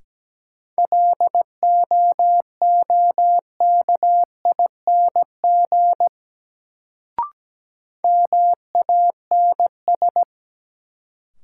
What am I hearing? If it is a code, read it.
Morse code, heard as a keyed tone, about 17 wpm, 700 Hz. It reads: LOOKING MANS
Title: Sentences from Top 1000 Words - Review Part 6 - New Word Drill - 17wpm